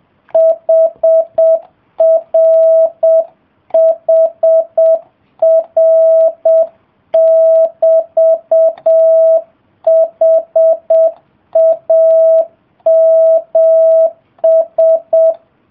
Morse code, heard as a keyed tone, about 7 wpm, 650 Hz. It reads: HR HR <BT> HAMS